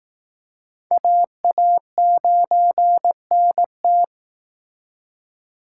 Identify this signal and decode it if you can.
Morse code, heard as a keyed tone, about 19 wpm, 700 Hz. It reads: AA9NT